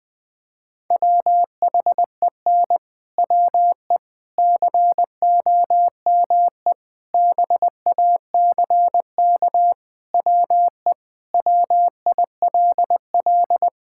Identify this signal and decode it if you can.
Morse code, heard as a keyed tone, about 20 wpm, 700 Hz. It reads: WHEN WE COME BACK WE WILL
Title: Sentences with 4-Word Phrases - 20wpm